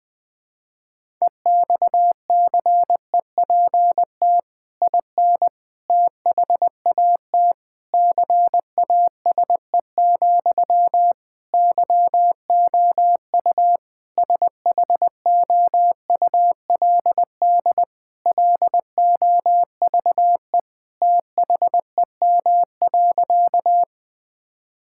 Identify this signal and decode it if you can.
Morse code, heard as a keyed tone, about 20 wpm, 700 Hz. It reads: EXCEPT IN THAT CASE, YOU SHOULD LOVE THEM.
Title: Sentences from Top 1600 Words 20wpm